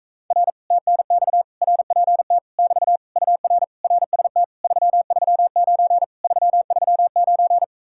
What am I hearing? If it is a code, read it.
Morse code, heard as a keyed tone, about 45 wpm, 700 Hz. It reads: R TNX RPT <BT> UR RST 339 339